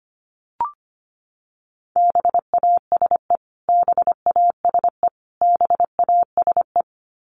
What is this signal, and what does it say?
Morse code, heard as a keyed tone, about 25 wpm, 700 Hz. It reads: BASE BASE BASE
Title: Top 500 Words Repeated 25wpm